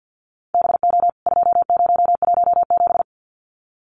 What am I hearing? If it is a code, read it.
Morse code, heard as a keyed tone, about 50 wpm, 700 Hz. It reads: BO 2017
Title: CWT Exchange 50wpm